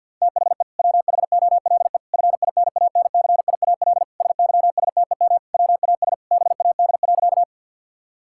Morse code, heard as a keyed tone, about 51 wpm, 700 Hz. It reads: THE WHOLE FINANCIAL SYSTEM WAS BAD.
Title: Sentences from Top 1000 Words - Review Part 5 - 50wpm